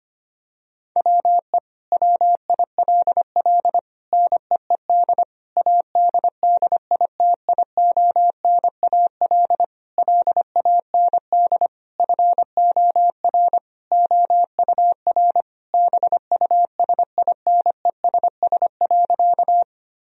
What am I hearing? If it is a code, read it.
Morse code, heard as a keyed tone, about 25 wpm, 700 Hz. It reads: WE WILL NEED ADDITIONAL LAND FOR OUR BUSINESS.
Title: Sentences from Top 1000 Words - Review Part 4 - 25wpm